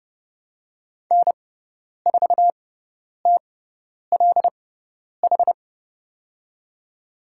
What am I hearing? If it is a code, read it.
Morse code, heard as a keyed tone, about 29 wpm, spaced as for 10 wpm, 700 Hz. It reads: N4TLH
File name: Lesson 17g - Letter u - Call Signs 30wpm